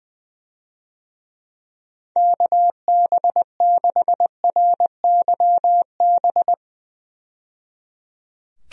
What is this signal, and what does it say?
Morse code, heard as a keyed tone, about 20 wpm, 700 Hz. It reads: KB6RYB
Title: USA Call Signs - 6 Characters - 20wpm